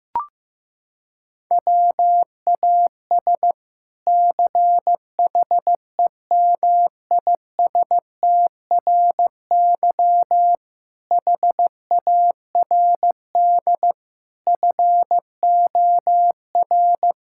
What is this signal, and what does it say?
Morse code, heard as a keyed tone, about 15 wpm, 700 Hz. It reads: WAS CHEMISTRY HARD FOR
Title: Sentences from Top 1000 Words - Review Part 6 - 15wpm